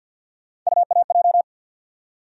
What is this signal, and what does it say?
Morse code, heard as a keyed tone, about 48 wpm, 700 Hz. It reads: UAJ
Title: USA Call Sign Postfixes - 3 Letters 50wpm